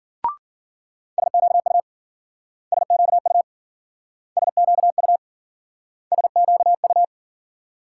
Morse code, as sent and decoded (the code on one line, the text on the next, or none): ... --.- ..- / ... --.- ..- / ... --.- ..- / ... --.- ..-
SQU SQU SQU SQU